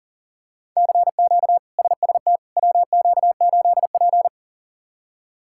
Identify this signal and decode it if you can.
Morse code, heard as a keyed tone, about 40 wpm, 700 Hz. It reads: CQ SST WQ8P
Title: K1USN SST QSO Practice - 40wpm